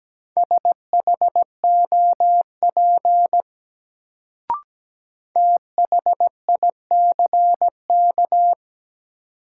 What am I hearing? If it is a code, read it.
Morse code, heard as a keyed tone, about 17 wpm, 700 Hz. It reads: SHOP THICK
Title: Sentences from Top 1600 Words - New Word Drill Alt - 17wpm